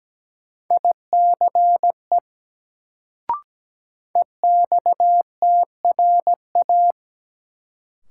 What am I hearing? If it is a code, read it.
Morse code, heard as a keyed tone, about 17 wpm, 700 Hz. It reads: ICE EXTRA
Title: Sentences from Top 1000 Words - New Word Drill - 17wpm